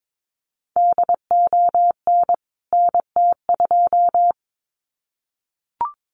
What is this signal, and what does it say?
Morse code, heard as a keyed tone, about 22 wpm, 700 Hz. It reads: DON NT2
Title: CWT Exchange 22wpm